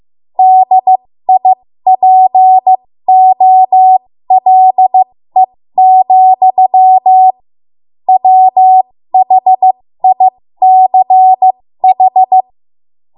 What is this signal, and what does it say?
Morse code, heard as a keyed tone, about 15 wpm, spaced as for 13 wpm, 750 Hz. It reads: DIPOLE, WHICH
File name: Morse / 13WPM